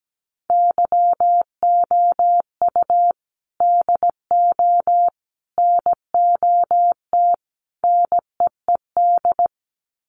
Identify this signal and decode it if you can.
Morse code, heard as a keyed tone, about 17 wpm, 700 Hz. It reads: YOU DO NOT NEED
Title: Sets of 4 Words 17wpm